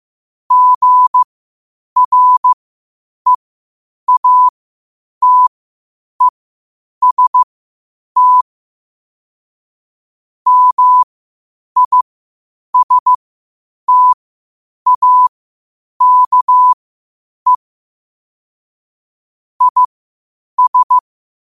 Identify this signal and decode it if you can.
Morse code, heard as a keyed tone, about 15 wpm, spaced as for 8 wpm, 1000 Hz. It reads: GREATEST MISTAKE IS
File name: qotdfiles / 8wpm